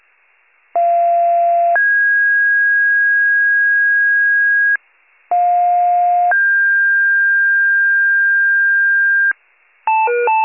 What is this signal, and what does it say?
Two-tone page, heard as a one-second tone, then a three-second tone, about 690 then 1650 Hz, about 710 then 1600 Hz. It is heard from 0.8 to 9.3 s.